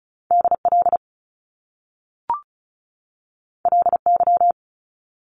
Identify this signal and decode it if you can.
Morse code, heard as a keyed tone, about 36 wpm, 700 Hz. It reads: DL LY
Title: Top Two Letter Combinations in Words 35wpm